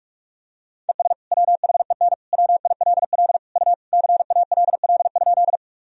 Morse code, heard as a keyed tone, about 44 wpm, 700 Hz. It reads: ES WHEN WILL U CALL?
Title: Sentences from Top 100 Words and Abbreviations 45wpm